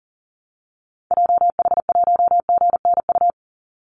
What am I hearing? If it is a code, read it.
Morse code, heard as a keyed tone, about 38 wpm, 700 Hz. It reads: JH1GNU
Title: Call Signs - 40wpm